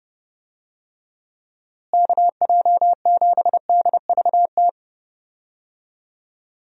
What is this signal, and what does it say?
Morse code, heard as a keyed tone, about 30 wpm, 700 Hz. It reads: KJ7DVT
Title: USA Call Signs - 6 Characters - Repeated 3x - 30wpm